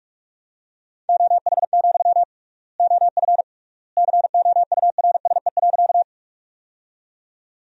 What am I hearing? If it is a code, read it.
Morse code, heard as a keyed tone, about 41 wpm, 700 Hz. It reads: OH, OF COURSE.
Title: 3x Word-Spacing - Sentences from Top 500 Words - 45wpm